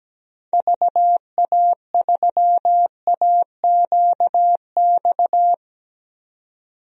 Morse code, heard as a keyed tone, about 17 wpm, 700 Hz. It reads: VA3AQX
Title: DXCC-100 Call Signs - 6 Characters - 17wpm